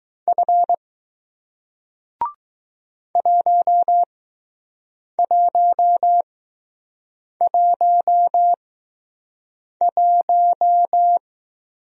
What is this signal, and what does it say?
Morse code, heard as a keyed tone, about 20 wpm, spaced as for 6 wpm, 700 Hz. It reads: F 1111